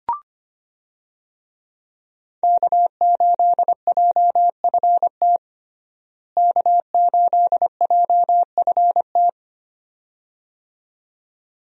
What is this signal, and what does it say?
Morse code, heard as a keyed tone, about 25 wpm, 700 Hz. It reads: K8JFT K8JFT
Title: USA Call Signs - 5 Characters - Repeated 2x - 25wpm